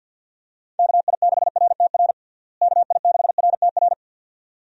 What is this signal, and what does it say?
Morse code, heard as a keyed tone, about 50 wpm, 700 Hz. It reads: KI6RTR KI6RTR
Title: USA Call Signs - 6 Characters - Repeated 2x - 50wpm